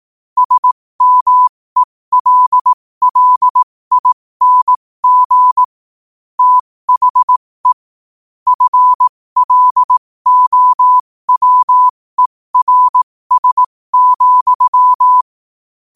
Morse code, heard as a keyed tone, about 18 wpm, spaced as for 15 wpm, 1000 Hz. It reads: SMELLING THE FLOWERS,
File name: qotdfiles / 15wpm